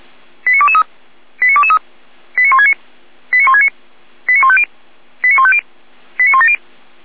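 Five-tone selective call (ZVEI), heard as quick runs of five tones, tones of about 70 ms. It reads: 89223 89223 89169 89169 89150 89150 89170
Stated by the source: ZVEI five-tone selective call received by SDR.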